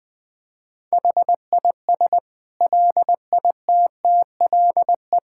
Morse code, heard as a keyed tone, about 20 wpm, 700 Hz. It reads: HIS LITTLE
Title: Sentences from Top 1600 Words 20wpm